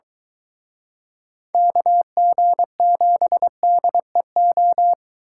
Morse code, heard as a keyed tone, about 23 wpm, 700 Hz. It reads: KG7DEO